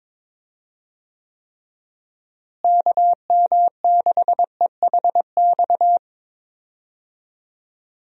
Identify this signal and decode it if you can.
Morse code, heard as a keyed tone, about 22 wpm, 700 Hz. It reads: KM6EHX
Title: USA Call Signs - 6 Characters - Repeated 2x - 22wpm